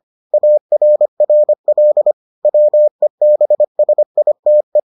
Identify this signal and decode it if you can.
Morse code, heard as a keyed tone, about 25 wpm, 600 Hz. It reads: ARRL WEBSITE